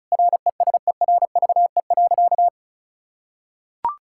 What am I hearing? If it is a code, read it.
Morse code, heard as a keyed tone, about 36 wpm, 700 Hz. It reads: RESERVE.
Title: Sentences from Top 1000 Words - Review Part 6 - 35wpm